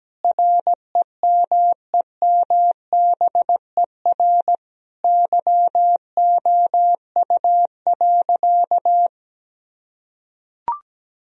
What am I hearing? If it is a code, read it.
Morse code, heard as a keyed tone, about 17 wpm, 700 Hz. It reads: REMEMBER YOU.
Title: Sentences from Top 1000 Words - Review Part 2 - 17wpm